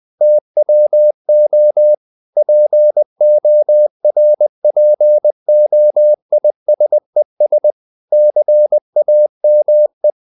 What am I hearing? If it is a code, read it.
Morse code, heard as a keyed tone, about 20 wpm, 600 Hz. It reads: TWO PORPOISES CAME
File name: the old man and the sea / the old man and the sea part 1 20wpm